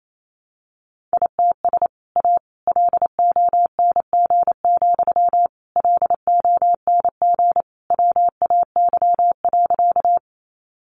Morse code, heard as a keyed tone, about 28 wpm, 700 Hz. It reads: ITS A LONG, LONG WAY.
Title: Sentences from Top 600 Words 28wpm